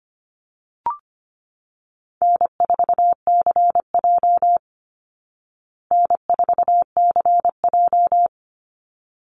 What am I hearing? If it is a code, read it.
Morse code, heard as a keyed tone, about 25 wpm, 700 Hz. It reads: N4CJ N4CJ
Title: Easy Call Signs Repeated 2x - 25wpm